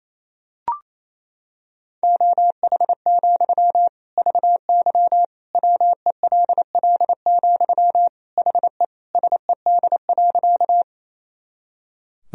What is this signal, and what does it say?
Morse code, heard as a keyed tone, about 28 wpm, 700 Hz. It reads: OH, VY WELL, HE SED.
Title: Sentences from Top 500 Words and Abbreviations 28wpm